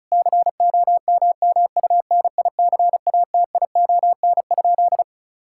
Morse code, heard as a keyed tone, about 36 wpm, 700 Hz. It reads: COMMUNICATION?
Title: Sentences from Top 1000 Words - Review Part 6 - 35wpm